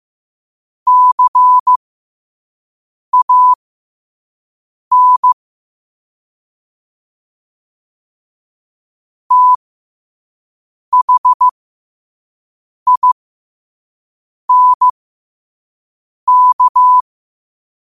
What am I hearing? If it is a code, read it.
Morse code, heard as a keyed tone, about 15 wpm, spaced as for 5 wpm, 1000 Hz. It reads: CAN THINK